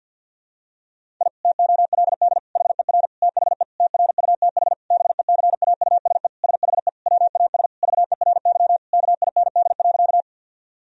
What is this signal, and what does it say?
Morse code, heard as a keyed tone, about 52 wpm, 700 Hz. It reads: I TOLD HER THE TRUTH BECAUSE SHE WAS VERY KIND.
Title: Sentences from Top 1000 Words - Review Part 3 - 50wpm